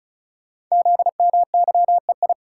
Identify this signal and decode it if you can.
Morse code, heard as a keyed tone, about 36 wpm, 700 Hz. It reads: ZMYEI
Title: Code Groups - 5 Letters - 35wpm